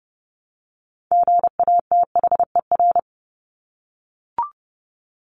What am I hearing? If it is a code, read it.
Morse code, heard as a keyed tone, about 30 wpm, 700 Hz. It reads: GATHER
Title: Top 1501-1750 Words 30wpm